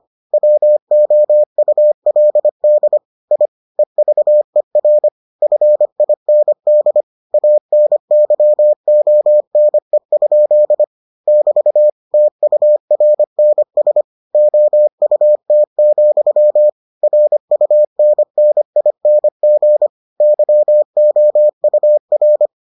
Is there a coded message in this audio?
Yes: Morse code from 0.3 to 22.5 s, heard as a keyed tone, about 25 wpm, 600 Hz. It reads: WOULD I EVER FIND ANYONE? <BT> TURNS OUT, RUNNING YOUR